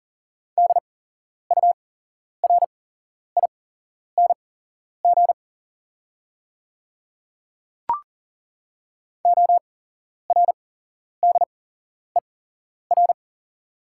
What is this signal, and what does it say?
Morse code, heard as a keyed tone, about 40 wpm, spaced as for 11 wpm, 690 Hz. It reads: DURING ORDER
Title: 8x Character Spacing - Top 500 Words - 40wpm